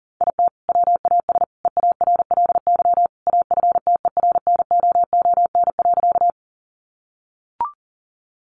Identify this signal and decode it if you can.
Morse code, heard as a keyed tone, about 40 wpm, 700 Hz. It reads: IT WAS EARLY AFTERNOON.
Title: Sentences from Top 1500 Words 40wpm